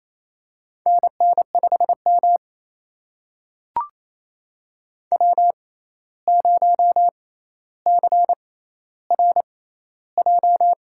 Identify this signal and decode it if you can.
Morse code, heard as a keyed tone, about 28 wpm, spaced as for 13 wpm, 700 Hz. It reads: NN5M W0CRJ
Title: USA Call Signs - 6x Character Spacing - 28wpm